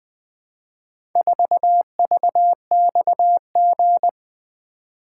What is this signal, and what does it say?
Morse code, heard as a keyed tone, about 20 wpm, 700 Hz. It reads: 4VXG
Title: Four Letter-Number Combinations - 20wpm